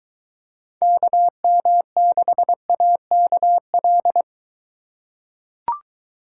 Morse code, heard as a keyed tone, about 23 wpm, 700 Hz. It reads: KM6AKL